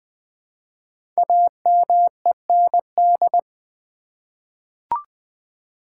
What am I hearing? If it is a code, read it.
Morse code, heard as a keyed tone, about 20 wpm, 700 Hz. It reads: AMEND
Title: 5 Letter Words - First 3 Letters Given - 20wpm